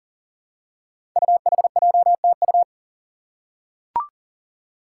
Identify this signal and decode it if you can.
Morse code, heard as a keyed tone, about 42 wpm, 700 Hz. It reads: UHJTU
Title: Five Challenging Letter-Number Combinations 40wpm